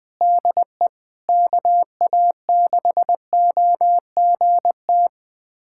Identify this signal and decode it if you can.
Morse code, heard as a keyed tone, about 21 wpm, 700 Hz. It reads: DE KA6OGT
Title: USA Callsign DE Callsign - 6 Characters - 20wpm